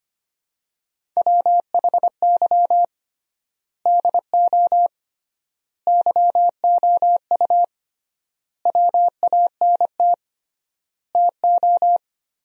Morse code, heard as a keyed tone, about 25 wpm, 700 Hz. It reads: WHY DO YOU WANT TO
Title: 3x Word-Spacing - Sentences from Top 200 Words - 25wpm